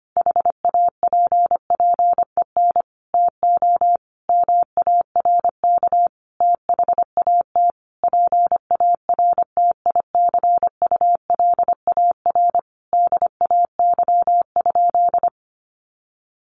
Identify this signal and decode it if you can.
Morse code, heard as a keyed tone, about 25 wpm, 700 Hz. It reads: HAPPEN TO MARK THAT PARTICULAR DAY?
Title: Sentences from Top 1400 Words 25wpm